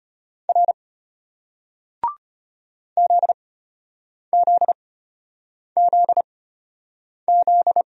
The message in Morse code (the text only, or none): R ZZZZ